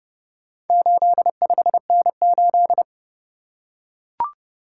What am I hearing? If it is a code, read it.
Morse code, heard as a keyed tone, about 28 wpm, 700 Hz. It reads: 85N8